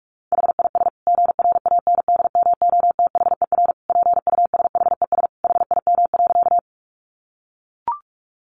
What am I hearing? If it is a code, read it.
Morse code, heard as a keyed tone, about 44 wpm, 700 Hz. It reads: HIS GRANDMOTHER PUSHES HIM.
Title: Additional Easy Phrases 45wpm